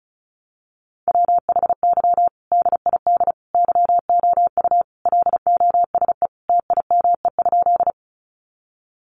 Morse code, heard as a keyed tone, about 34 wpm, 700 Hz. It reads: WHY DID YOU LOSE TIME?
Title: Sentences from Top 1100 Words 35wpm